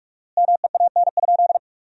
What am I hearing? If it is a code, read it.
Morse code, heard as a keyed tone, about 45 wpm, 700 Hz. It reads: MEAN?
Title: 4x Word-Spacing - Sentences from Top 200 Words - 45wpm